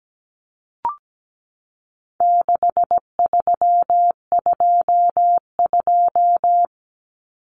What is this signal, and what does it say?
Morse code, heard as a keyed tone, about 17 wpm, 700 Hz. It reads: 6322
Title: Numbers - 4-Digit 17wpm